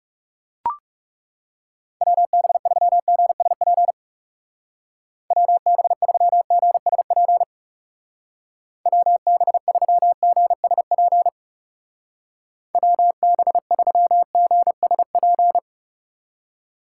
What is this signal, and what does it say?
Morse code, heard as a keyed tone, about 37 wpm, 700 Hz. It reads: WB3GSP WB3GSP WB3GSP WB3GSP
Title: Speed Racing - USA Call Signs 30wpm